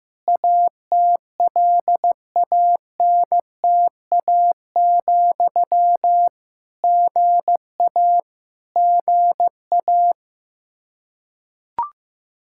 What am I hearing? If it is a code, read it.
Morse code, heard as a keyed tone, about 15 wpm, 700 Hz. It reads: ATLANTA, GA GA
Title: Practice QSO - Exchange 1 - Beginning - 15wpm